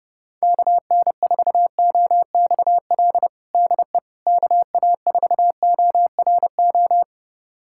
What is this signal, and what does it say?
Morse code, heard as a keyed tone, about 30 wpm, 700 Hz. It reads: KN4OXL DE KA4ORO
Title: USA Callsign DE Callsign 30wpm